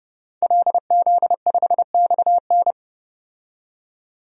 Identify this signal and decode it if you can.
Morse code, heard as a keyed tone, about 30 wpm, 700 Hz. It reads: LZ5XN
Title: DXCC-100 Call Signs - 5 Characters - 30wpm